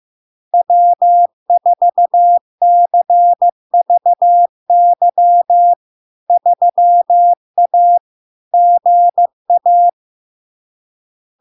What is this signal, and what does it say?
Morse code, heard as a keyed tone, about 15 wpm, 700 Hz. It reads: W4CVY 3A GA
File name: LARK-FD-2022-Log-in-CW-15wpm-700-Hz